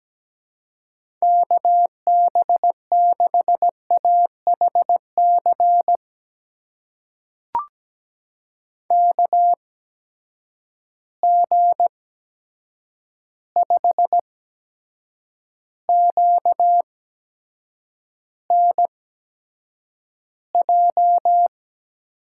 Morse code, heard as a keyed tone, about 17 wpm, spaced as for 7 wpm, 700 Hz. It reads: KB6AHC KG5QNJ
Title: USA Call Signs - 6 Characters - 8x Character Spacing - 17wpm